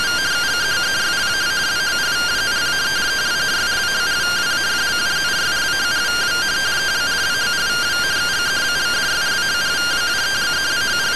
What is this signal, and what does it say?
Two-tone teleprinter signal, about 170 Hz shift.